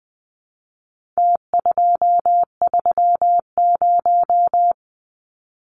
Morse code, heard as a keyed tone, about 20 wpm, 700 Hz. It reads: T230
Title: Numbers - Contest Style - 4-Digit 20wpm